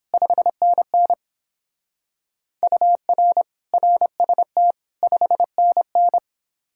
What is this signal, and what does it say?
Morse code, heard as a keyed tone, about 29 wpm, 700 Hz. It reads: 5NN UR RST 5NN